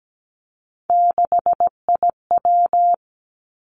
Morse code, heard as a keyed tone, about 17 wpm, 700 Hz. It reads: 6IW
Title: Three Challenging Letter-Number Combinations 17wpm